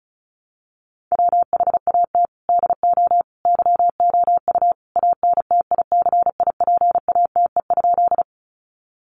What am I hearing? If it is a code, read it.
Morse code, heard as a keyed tone, about 34 wpm, 700 Hz. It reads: WHAT DO YOU ANTICIPATE?
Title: Sentences from Top 1500 Words 35wpm